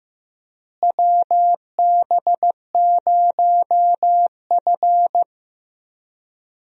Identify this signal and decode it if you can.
Morse code, heard as a keyed tone, about 16 wpm, 700 Hz. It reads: WB0F
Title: USA Call Signs - 4 Characters - Repeated 2x - 15wpm